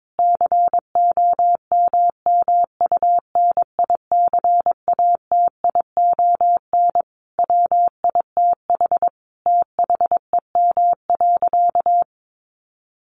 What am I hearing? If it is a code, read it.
Morse code, heard as a keyed tone, about 22 wpm, 700 Hz. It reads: COMMUNICATION WITH THEM.
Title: Sentences from Top 1400 Words 22wpm